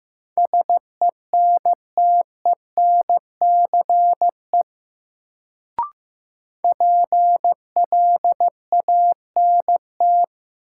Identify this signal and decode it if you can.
Morse code, heard as a keyed tone, about 15 wpm, 700 Hz. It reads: SENTENCE PLANT
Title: Sentences from Top 1000 Words - Review Part 1 - New Word Drill - 15wpm